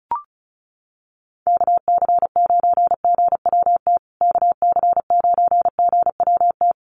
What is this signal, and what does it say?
Morse code, heard as a keyed tone, about 34 wpm, 700 Hz. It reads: KC9GWT KC9GWT
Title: USA Call Signs Repeated 3x 35wpm